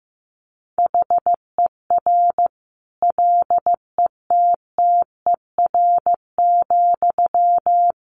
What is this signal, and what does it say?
Morse code, heard as a keyed tone, about 15 wpm, 700 Hz. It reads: HER LETTER,
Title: Sentences from Top 1100 Words 15wpm